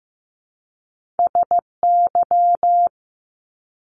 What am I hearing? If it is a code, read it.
Morse code, heard as a keyed tone, about 15 wpm, 700 Hz. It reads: SY
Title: Top Two Letter Combinations in Words 15wpm